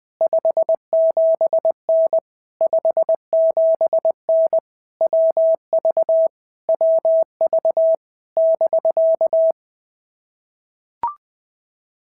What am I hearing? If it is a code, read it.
Morse code, heard as a keyed tone, about 20 wpm, 650 Hz. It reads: 57N 57N WV WV <BK>
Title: POTA QSO Practice - 20wpm